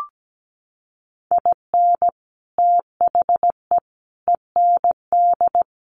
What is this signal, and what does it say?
Morse code, heard as a keyed tone, about 17 wpm, 700 Hz. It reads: IN THE END
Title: Sets of 3 Words 17wpm